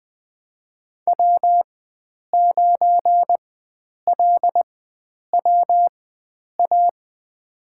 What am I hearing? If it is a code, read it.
Morse code, heard as a keyed tone, about 20 wpm, spaced as for 9 wpm, 700 Hz. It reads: W9LWA
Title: USA Call Signs - 4x Character Spacing - 20wpm